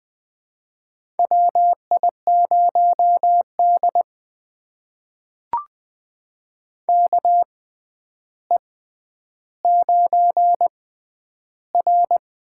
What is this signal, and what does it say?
Morse code, heard as a keyed tone, about 20 wpm, spaced as for 10 wpm, 700 Hz. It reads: WI0D KE9R
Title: USA Call Signs - 6x Character Spacing - 20wpm